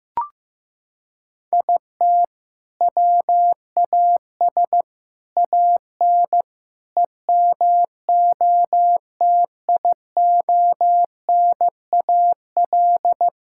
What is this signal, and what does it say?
Morse code, heard as a keyed tone, about 15 wpm, 700 Hz. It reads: IT WAS AN EMOTIONAL
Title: Sentences from Top 1000 Words - Review Part 6 - 15wpm